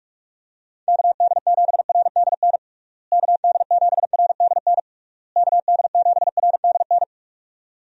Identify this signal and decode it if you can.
Morse code, heard as a keyed tone, about 46 wpm, 700 Hz. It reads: KD7RDN KD7RDN KD7RDN
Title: USA Call Signs - 6 Characters - Repeated 3x - 45wpm